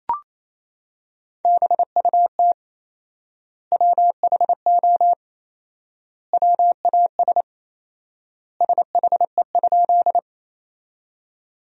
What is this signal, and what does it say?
Morse code, heard as a keyed tone, about 28 wpm, 700 Hz. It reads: BUT WHO WAS SHE?
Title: 4x Word-Spacing - Sentences from Top 100 Words - 28wpm